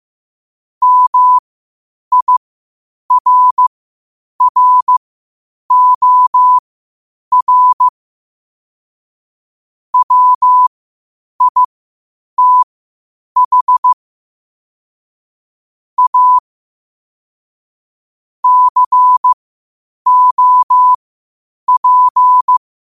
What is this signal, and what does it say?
Morse code, heard as a keyed tone, about 15 wpm, spaced as for 8 wpm, 1000 Hz. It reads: MIRROR WITH A COP
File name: qotdfiles / 8wpm